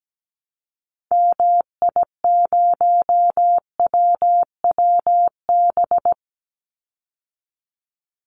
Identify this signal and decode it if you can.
Morse code, heard as a keyed tone, about 17 wpm, 700 Hz. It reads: MI0WWB
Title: Call Signs Repeated 3x - 17wpm